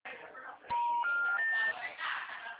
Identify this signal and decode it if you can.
Special information tone, with speech.
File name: out- -2018- - - .2704